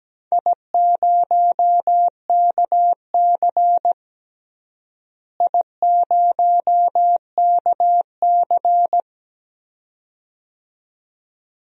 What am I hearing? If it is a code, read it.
Morse code, heard as a keyed tone, about 17 wpm, 700 Hz. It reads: I0KC I0KC
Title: DXCC-100 Call Signs - 4 Characters - Repeated 2x - 17wpm